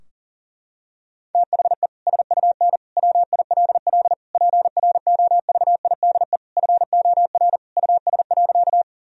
Morse code, heard as a keyed tone, about 40 wpm, 700 Hz. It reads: THE SUN WILL PROVIDE FOR US.
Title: Sentences from Top 1000 Words - Review Part 3 - 40wpm